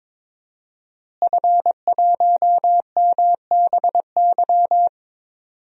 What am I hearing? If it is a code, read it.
Morse code, heard as a keyed tone, about 22 wpm, 700 Hz. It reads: F1MBY
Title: DXCC-100 Call Signs - 5 Characters - Repeated 3x - 22wpm